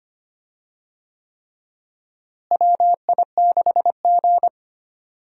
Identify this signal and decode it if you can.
Morse code, heard as a keyed tone, about 25 wpm, 700 Hz. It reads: WI6G
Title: USA Call Signs - 4 Characters - Repeated 3x - 25wpm